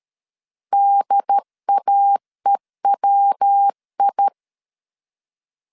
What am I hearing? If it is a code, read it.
Morse code, heard as a keyed tone, about 13 wpm, 800 Hz. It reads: DAEWI
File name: CW / DET / 1 3 1500C S C 12wpm 2750